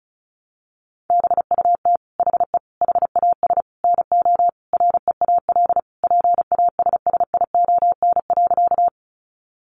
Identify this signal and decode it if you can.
Morse code, heard as a keyed tone, about 34 wpm, 700 Hz. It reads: BUT HE HAS NO REAL PASSION.
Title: Sentences from Top 1300 Words 35wpm